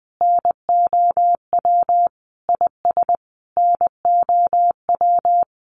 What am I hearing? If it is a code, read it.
Morse code, heard as a keyed tone, about 20 wpm, 700 Hz. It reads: NOW IS NOW
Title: Sentences from Top 700 Words 20wpm